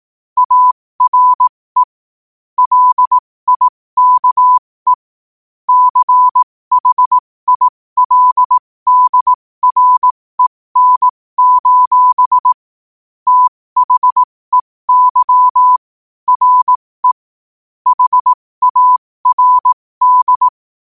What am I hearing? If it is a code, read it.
Morse code, heard as a keyed tone, about 18 wpm, spaced as for 15 wpm, 1000 Hz. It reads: ARE LIKE CHILDREN: THEY RE HARD
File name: qotdfiles / 15wpm